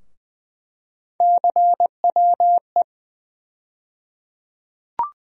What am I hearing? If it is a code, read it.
Morse code, heard as a keyed tone, about 20 wpm, 700 Hz. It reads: CWE
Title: USA Call Sign Postfixes - 3 Letters 20wpm